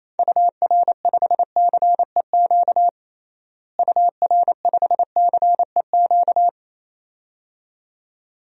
Morse code, heard as a keyed tone, about 27 wpm, 700 Hz. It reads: UR5CEQ UR5CEQ
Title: DXCC-100 Call Signs - 6 Characters - Repeated 2x - 28wpm